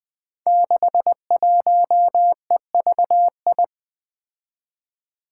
Morse code, heard as a keyed tone, about 20 wpm, 700 Hz. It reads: 61EVI
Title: Five Challenging Letter-Number Combinations 20wpm